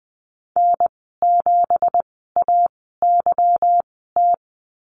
Morse code, heard as a keyed tone, about 20 wpm, spaced as for 14 wpm, 700 Hz. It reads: N7AYT
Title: 179 Lesson 29g - Slash - Call Signs 20wpm